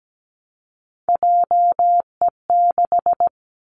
Morse code, heard as a keyed tone, about 17 wpm, 700 Hz. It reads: JE6
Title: Three Challenging Letter-Number Combinations 17wpm